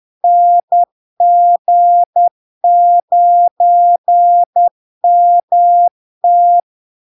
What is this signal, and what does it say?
Morse code, heard as a keyed tone, about 10 wpm, 700 Hz. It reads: NG9MT
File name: LARK-FD-2022-Log-in-CW-10wpm-700-Hz